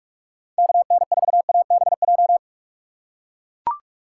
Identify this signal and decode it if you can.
Morse code, heard as a keyed tone, about 42 wpm, 700 Hz. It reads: KN4ABJ
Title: USA Call Signs - 6 Characters - 4x Character Spacing - 45wpm